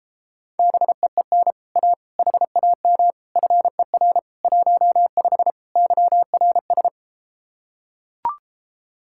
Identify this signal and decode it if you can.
Morse code, heard as a keyed tone, about 34 wpm, 700 Hz. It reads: BEEN A HAM FER 15 YRS